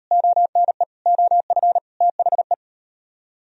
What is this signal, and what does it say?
Morse code, heard as a keyed tone, about 39 wpm, 700 Hz. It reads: ONE OF THE